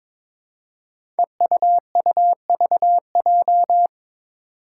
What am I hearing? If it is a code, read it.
Morse code, heard as a keyed tone, about 22 wpm, 700 Hz. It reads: EUUVJ
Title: Five Challenging Letter-Number Combinations 22wpm